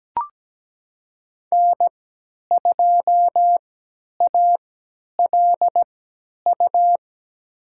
Morse code, heard as a keyed tone, about 17 wpm, spaced as for 10 wpm, 700 Hz. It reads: N2ALU
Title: USA Call Signs - 5 Characters - 3x Character Spacing - 17wpm